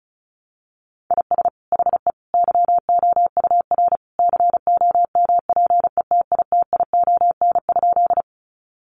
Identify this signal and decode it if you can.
Morse code, heard as a keyed tone, about 34 wpm, 700 Hz. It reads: IS HE YOUR COMPETITION?
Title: Sentences from Top 1000 Words 35wpm